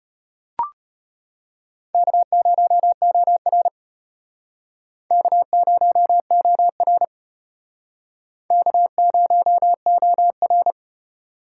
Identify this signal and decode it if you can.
Morse code, heard as a keyed tone, about 31 wpm, 700 Hz. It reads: K0OR K0OR K0OR
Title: Speed Racing - USA 1x2 Call Signs 25wpm